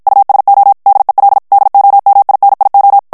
Morse code, heard as a keyed tone, about 54 wpm, 800 Hz. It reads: USO DEL DOMINIO
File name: ESPAGNOL TEXTE EN CLAIR 2 VITESSE 50 MOTS MINUTES